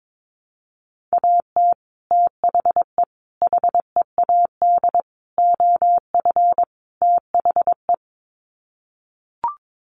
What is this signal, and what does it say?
Morse code, heard as a keyed tone, about 22 wpm, 700 Hz. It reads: AT THE HEAD OF THE
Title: Sets of 5 Words 22wpm